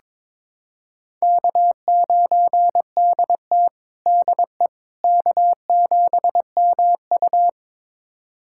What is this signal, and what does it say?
Morse code, heard as a keyed tone, about 22 wpm, 700 Hz. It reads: K9DT DE K7MU
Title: USA Callsign DE Callsign - 4 Characters - 22wpm